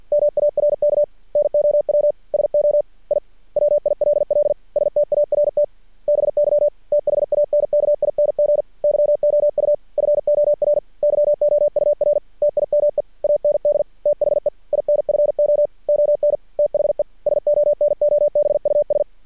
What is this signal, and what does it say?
Morse code, heard as a keyed tone, about 50 wpm, 590 Hz. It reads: MARK NOW SO I WILL START BY THANKING YOU FOR YOUR TIME AND THE INFO ON THE SONOBUS
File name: archived / CRSnet